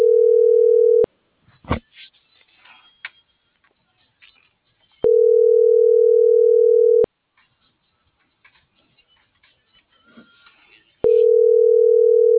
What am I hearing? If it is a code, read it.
Ringback tone (North America), 440 + 480 Hz.